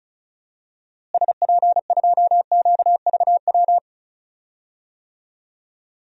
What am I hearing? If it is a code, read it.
Morse code, heard as a keyed tone, about 36 wpm, 700 Hz. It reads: SP2QVW
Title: DXCC-100 Call Signs - 6 Characters - Repeated 3x - 35wpm